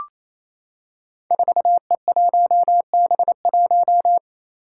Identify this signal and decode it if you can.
Morse code, heard as a keyed tone, about 28 wpm, 700 Hz. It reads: 4E1B1
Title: Five Challenging Letter-Number Combinations 28wpm